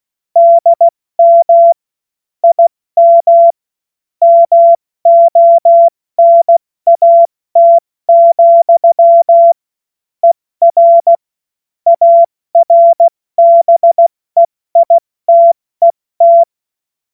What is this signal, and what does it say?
Morse code, heard as a keyed tone, about 16 wpm, 680 Hz. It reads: DM IM MONAT, ER ARBEITET
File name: Witz-14-Oktober-2025-70-F80